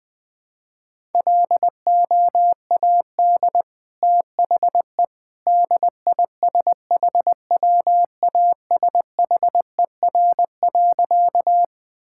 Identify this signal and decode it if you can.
Morse code, heard as a keyed tone, about 21 wpm, 700 Hz. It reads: LOAD THE DISHWASHER.
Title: Easy Sentences with Hints - 20wpm